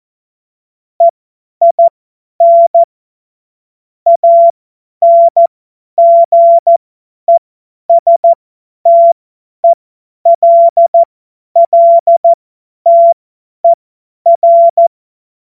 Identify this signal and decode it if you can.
Morse code, heard as a keyed tone, about 14 wpm, spaced as for 10 wpm, 680 Hz. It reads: EIN ANGESTELLTER